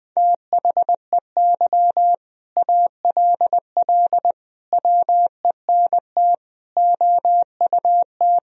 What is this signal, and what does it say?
Morse code, heard as a keyed tone, about 20 wpm, 700 Hz. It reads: THEY ALL WENT OUT
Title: Sentences from Top 1000 Words - Review Part 1 - 20wpm